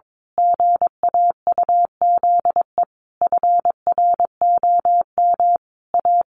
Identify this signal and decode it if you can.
Morse code, heard as a keyed tone, about 22 wpm, 700 Hz. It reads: GAUZE FROM A
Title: A Sound Of Thunder 22wpm